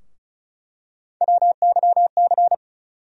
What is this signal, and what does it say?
Morse code, heard as a keyed tone, about 34 wpm, 700 Hz. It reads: WYC
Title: USA Call Sign Postfixes - 3 Letters 35wpm